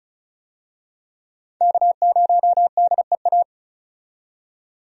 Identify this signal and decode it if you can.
Morse code, heard as a keyed tone, about 34 wpm, 700 Hz. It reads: K0DEA
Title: USA Call Signs - 5 Characters - 35wpm